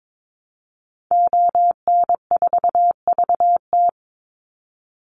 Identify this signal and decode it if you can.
Morse code, heard as a keyed tone, about 22 wpm, 700 Hz. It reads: ON4VT
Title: Call Signs Repeated 3x - 22wpm